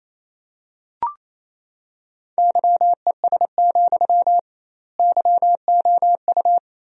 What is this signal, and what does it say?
Morse code, heard as a keyed tone, about 28 wpm, 700 Hz. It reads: YES, YOU
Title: 2x Word-Spacing - Sentences from Top 500 Words - 28wpm